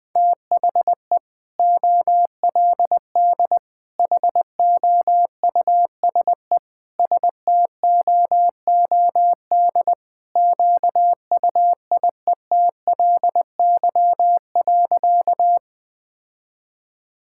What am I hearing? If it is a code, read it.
Morse code, heard as a keyed tone, about 20 wpm, 700 Hz. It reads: THE OLD HOUSE STOOD QUIETLY.
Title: Sentences with 2-Word Phrases - 20wpm